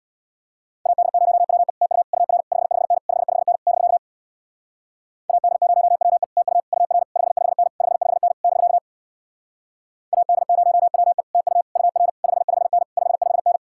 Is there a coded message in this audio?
Yes: Morse code from 0.9 to 13.6 s, heard as a keyed tone, about 65 wpm, 700 Hz. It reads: AD0WE TU UR 55N 55N <BK> AD0WE TU UR 55N 55N <BK> AD0WE TU UR 55N 55N